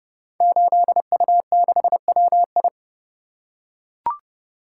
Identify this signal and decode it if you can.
Morse code, heard as a keyed tone, about 29 wpm, 700 Hz. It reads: 8U6WI
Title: Code Groups - 5 Letter:Numbers - 30wpm